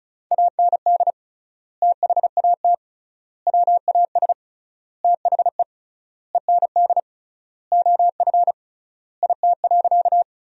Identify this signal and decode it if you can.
Morse code, heard as a keyed tone, about 34 wpm, 700 Hz. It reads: AND THAT WAS THE END OF IT.
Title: 3x Word-Spacing - Sentences from Top 200 Words - 35wpm